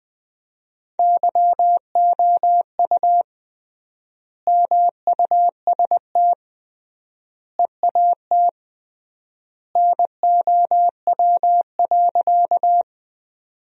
Morse code, heard as a keyed tone, about 20 wpm, 700 Hz. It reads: YOU MUST EAT NOW.
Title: 3x Word-Spacing - Sentences from Top 300 Words - 20wpm